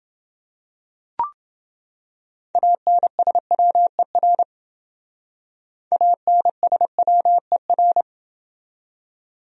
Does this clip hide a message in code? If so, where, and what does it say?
Yes: Morse code from 2.5 to 8.0 s, heard as a keyed tone, about 29 wpm, 700 Hz. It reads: ANSWER ANSWER